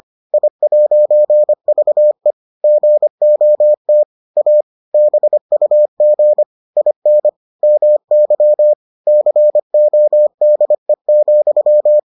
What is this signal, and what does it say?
Morse code, heard as a keyed tone, about 25 wpm, 600 Hz. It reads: I'VE GOT A BUG IN MY CODE,